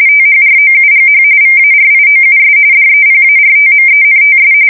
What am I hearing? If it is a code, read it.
RTTY, about 45.45 baud. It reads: THROUGH VARIOUS DEPARTMENTS
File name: fsk170 / battleshipnj